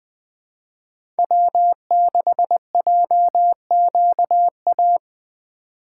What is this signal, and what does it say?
Morse code, heard as a keyed tone, about 20 wpm, 700 Hz. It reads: W6JQA
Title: USA Call Signs - 5 Characters - 4x Character Spacing - 20wpm